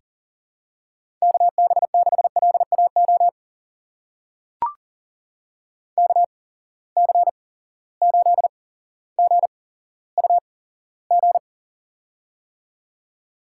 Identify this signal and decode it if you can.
Morse code, heard as a keyed tone, about 40 wpm, spaced as for 17 wpm, 700 Hz. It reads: KB6LAO KC8GUG
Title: USA Call Signs - 6 Characters - 8x Character Spacing - 40wpm